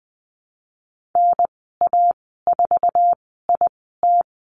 Morse code, heard as a keyed tone, about 20 wpm, spaced as for 15 wpm, 700 Hz. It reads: NA4IT
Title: 161 Lesson 26g - Letter g - Call Signs 20wpm